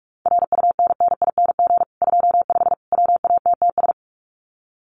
Morse code, heard as a keyed tone, about 44 wpm, 700 Hz. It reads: RUNNING 25 WATTS
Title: Common QSO Phrases 45wpm